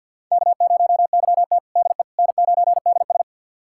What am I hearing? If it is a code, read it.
Morse code, heard as a keyed tone, about 50 wpm, 700 Hz. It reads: K0YT DE N9DS
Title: USA Callsign DE Callsign - 4 Characters - 50wpm